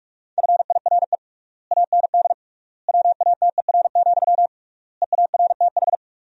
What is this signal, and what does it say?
Morse code, heard as a keyed tone, about 46 wpm, 700 Hz. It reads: FIRE AND WATER, EARTH
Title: 3x Word-Spacing - Sentences from Top 400 Words - 45wpm